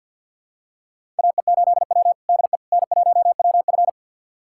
Morse code, heard as a keyed tone, about 51 wpm, 700 Hz. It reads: AE8W DE N1WF
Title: USA Callsign DE Callsign - 4 Characters - 50wpm